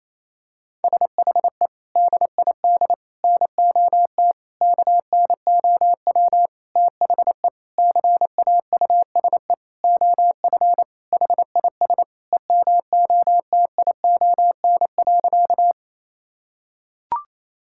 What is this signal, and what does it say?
Morse code, heard as a keyed tone, about 28 wpm, 700 Hz. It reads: SHE DID NOT KNOW THE CAUSE OF HIS EMOTION.
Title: Sentences from Top 1600 Words 28wpm